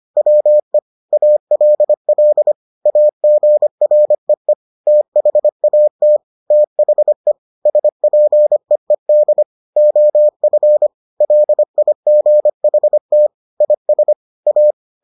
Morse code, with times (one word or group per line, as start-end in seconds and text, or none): 0.2-0.8 WE
1.1-2.5 ALL
2.9-4.5 AGREE
4.9-6.2 THAT
6.5-7.3 THE
7.7-9.4 SPEED
9.8-10.9 OF
11.2-13.3 LIGHT
13.6-14.1 IS
14.5-14.7 A